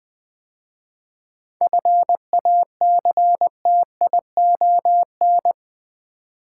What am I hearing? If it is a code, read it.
Morse code, heard as a keyed tone, about 20 wpm, 690 Hz. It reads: FACTION
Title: 7 Letter Words - First 3 Letters Given - 20wpm